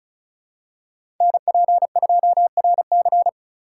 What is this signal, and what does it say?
Morse code, heard as a keyed tone, about 36 wpm, 700 Hz. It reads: NP2RC
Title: USA Call Signs - 5 Characters - 3x Character Spacing - 35wpm